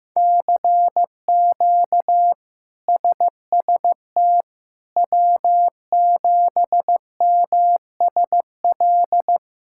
Morse code, heard as a keyed tone, about 15 wpm, 700 Hz. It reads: CQ SST W7MSL
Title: K1USN SST QSO Practice - 15wpm